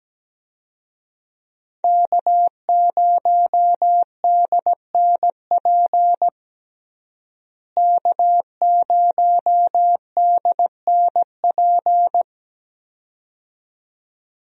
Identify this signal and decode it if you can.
Morse code, heard as a keyed tone, about 17 wpm, 700 Hz. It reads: K0DNP K0DNP
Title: USA Call Signs - 5 Characters - Repeated 2x - 17wpm